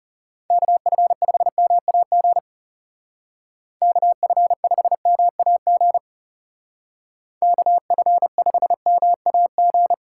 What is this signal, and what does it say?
Morse code, heard as a keyed tone, about 39 wpm, 700 Hz. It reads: KF5MAG KF5MAG KF5MAG